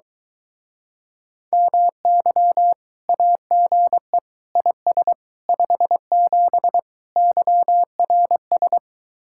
Morse code, heard as a keyed tone, about 23 wpm, 700 Hz. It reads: MY AGE IS 57 YRS